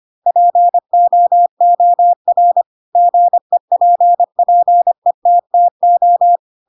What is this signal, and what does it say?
Morse code, heard as a keyed tone, about 25 wpm, 700 Hz. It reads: POOR GEPPETTO
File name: Pinocchio / Pinocchio ch3 25 25 ik1hge